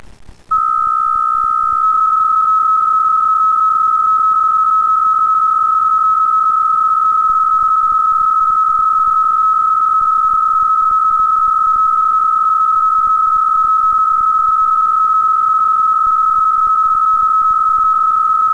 PSK63, about 62.5 baud.